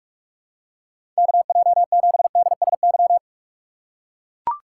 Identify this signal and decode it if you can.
Morse code, heard as a keyed tone, about 41 wpm, 700 Hz. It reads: KJ7DSY
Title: USA Call Signs - 6 Characters - 2x Character Spacing - 45wpm